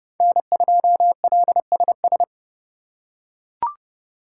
Morse code, heard as a keyed tone, about 27 wpm, 700 Hz. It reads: N2LSS